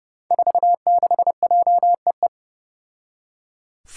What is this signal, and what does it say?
Morse code, heard as a keyed tone, about 29 wpm, 700 Hz. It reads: 46JEE
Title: Five Challenging Letter-Number Combinations 30wpm